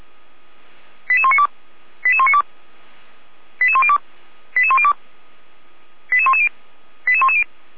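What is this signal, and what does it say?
Five-tone selective call (ZVEI), heard as quick runs of five tones, tones of about 70 ms. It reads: 88192 88192 88102 88102 88110 88110
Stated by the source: ZVEI five-tone call received with an SDR, recorded by Feuerwehr Orlamünde.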